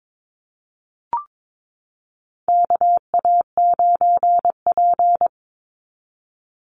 Morse code, heard as a keyed tone, about 22 wpm, 700 Hz. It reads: KA9P
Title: Call Signs Repeated 2x - 22wpm